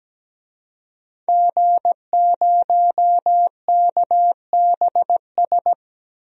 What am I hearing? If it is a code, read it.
Morse code, heard as a keyed tone, about 17 wpm, 700 Hz. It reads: G0KBS
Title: DXCC-100 Call Signs - 5 Characters - 17wpm